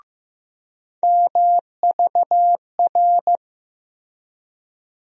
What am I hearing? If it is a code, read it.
Morse code, heard as a keyed tone, about 15 wpm, 700 Hz. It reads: MVR
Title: USA Call Sign Postfixes - 3 Letters 15wpm